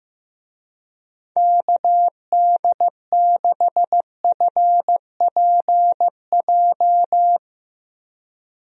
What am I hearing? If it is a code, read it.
Morse code, heard as a keyed tone, about 15 wpm, 700 Hz. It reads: KD6FPJ
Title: USA Call Signs - 6 Characters - Repeated 3x - 15wpm